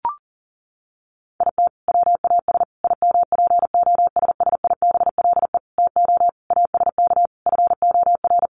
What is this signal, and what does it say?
Morse code, heard as a keyed tone, about 40 wpm, 700 Hz. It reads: IT WAS IMPOSSIBLE TO ASK FOR
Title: Sentences from Top 1100 Words 40wpm